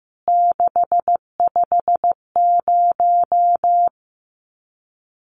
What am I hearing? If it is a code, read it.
Morse code, heard as a keyed tone, about 15 wpm, 700 Hz. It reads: 650
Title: CWT Exchange 15wpm